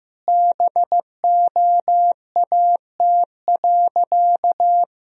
Morse code, heard as a keyed tone, about 15 wpm, 700 Hz. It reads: BOAT.
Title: Sentences from Top 1000 Words - Review Part 4 - 15wpm